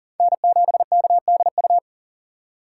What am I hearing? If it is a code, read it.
Morse code, heard as a keyed tone, about 40 wpm, 700 Hz. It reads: N7KDU
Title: USA Call Signs - 5 Characters - 4x Character Spacing - 40wpm